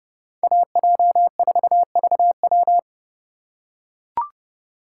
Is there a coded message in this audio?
Yes: Morse code from 0.4 to 2.8 s, heard as a keyed tone, about 29 wpm, 700 Hz. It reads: AJ4VW